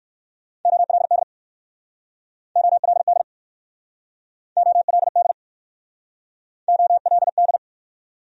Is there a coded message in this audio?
Yes: Morse code from 0.7 to 7.6 s, heard as a keyed tone, about 63 wpm, 700 Hz. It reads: OLD OLD OLD OLD